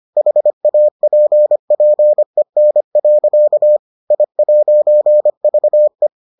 Morse code, heard as a keyed tone, about 25 wpm, 600 Hz. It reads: HAPPEN. I'VE